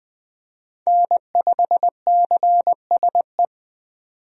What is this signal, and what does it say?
Morse code, heard as a keyed tone, about 20 wpm, 700 Hz. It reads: N5CSE
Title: USA Call Signs - 5 Characters - 20wpm